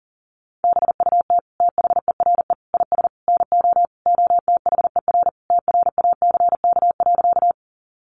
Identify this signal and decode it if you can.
Morse code, heard as a keyed tone, about 40 wpm, 700 Hz. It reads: BUT THERE IS NO OTHER TRACK.
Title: Sentences from Top 1100 Words 40wpm